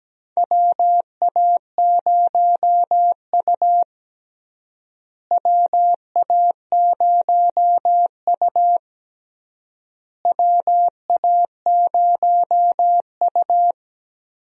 Morse code, heard as a keyed tone, about 17 wpm, 700 Hz. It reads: WA0U WA0U WA0U
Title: USA Call Signs - 4 Characters - Repeated 3x - 17wpm